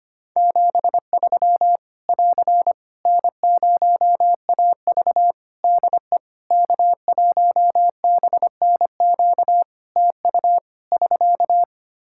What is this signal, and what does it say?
Morse code, heard as a keyed tone, about 25 wpm, 700 Hz. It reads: 73 <AR> N0AV DE K1BNQ TU <SK>
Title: Practice QSO - Exchange 4 - Ending - 25wpm